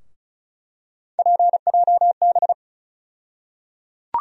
Morse code, heard as a keyed tone, about 36 wpm, 700 Hz. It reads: PJB
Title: USA Call Sign Postfixes - 3 Letters 35wpm